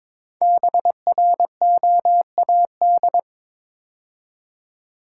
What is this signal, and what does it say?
Morse code, heard as a keyed tone, about 22 wpm, 700 Hz. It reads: BROAD
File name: 5 Letter Words - First 3 Letters Given - 22wpm